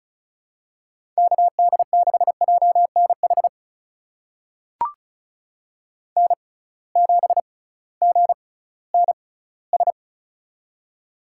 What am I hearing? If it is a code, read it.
Morse code, heard as a keyed tone, about 36 wpm, spaced as for 19 wpm, 700 Hz. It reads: KD6JNH N7GNS
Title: USA Call Signs - 6x Character Spacing - 35wpm